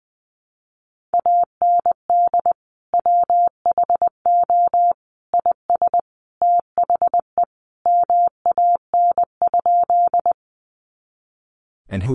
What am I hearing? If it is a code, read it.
Morse code, heard as a keyed tone, about 20 wpm, 700 Hz. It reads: AND WHO IS THE MAN?
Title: Sentences from Top 200 Words 20wpm